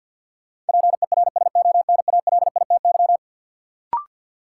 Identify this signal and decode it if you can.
Morse code, heard as a keyed tone, about 52 wpm, 700 Hz. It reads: PERSONALITY
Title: Sentences from Top 1400 Words - New Word Drill - 50wpm